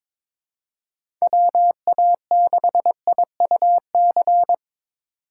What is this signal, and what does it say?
Morse code, heard as a keyed tone, about 22 wpm, 700 Hz. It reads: WA6IUC
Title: USA Call Signs - 2x Character Spacing - 22wpm